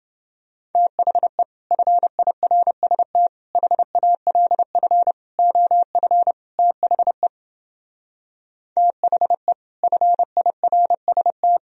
Morse code, heard as a keyed tone, about 29 wpm, 700 Hz. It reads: THE FIRST HALF OF THE THE FIRST